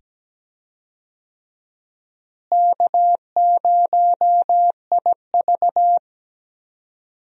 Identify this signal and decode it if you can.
Morse code, heard as a keyed tone, about 17 wpm, 700 Hz. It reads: K0IV
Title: USA Call Signs - 4 Characters - Repeated 2x - 17wpm